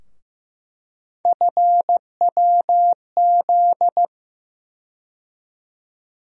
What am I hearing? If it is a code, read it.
Morse code, heard as a keyed tone, about 15 wpm, 700 Hz. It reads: FWZ
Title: USA Call Sign Postfixes - 3 Letters 15wpm